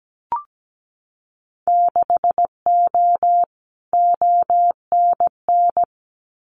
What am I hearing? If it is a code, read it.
Morse code, heard as a keyed tone, about 17 wpm, 700 Hz. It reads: 6O ONN
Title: Winter Field Day Exchanges Encore 17wpm